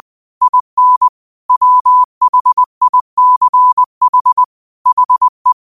Morse code, heard as a keyed tone, about 20 wpm, 1000 Hz. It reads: IN WHICH HE
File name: 20wpm